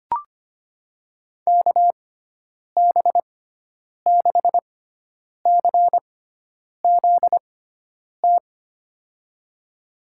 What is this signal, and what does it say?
Morse code, heard as a keyed tone, about 25 wpm, spaced as for 9 wpm, 700 Hz. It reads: KB6CZT
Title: USA Call Signs - 6 Characters - 6x Character Spacing - 25wpm